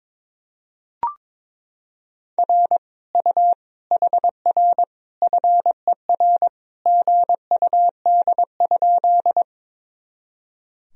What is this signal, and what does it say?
Morse code, heard as a keyed tone, about 21 wpm, 700 Hz. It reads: R U HR FER GUD?
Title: Sentences from Top 200 Words and Abbreviations 22wpm